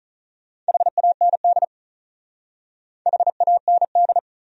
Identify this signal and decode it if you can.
Morse code, heard as a keyed tone, about 44 wpm, 700 Hz. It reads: HAND HAND
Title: Speed-Racing Top 500 Words 35wpm